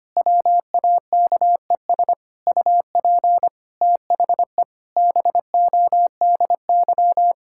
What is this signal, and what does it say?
Morse code, heard as a keyed tone, about 25 wpm, 700 Hz. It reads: WAKES UP THE BODY
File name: practice / 25wpm / Difficult Sentences with Hints - 25wpm